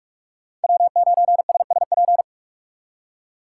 Morse code, heard as a keyed tone, about 64 wpm, spaced as for 54 wpm, 700 Hz. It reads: W9SSP
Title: Speed Racing - USA 1x3 Call Signs 45wpm